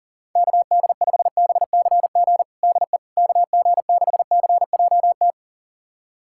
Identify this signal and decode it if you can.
Morse code, heard as a keyed tone, about 40 wpm, 700 Hz. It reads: KD5BCG DE KG6CJT
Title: USA Callsign DE Callsign 40wpm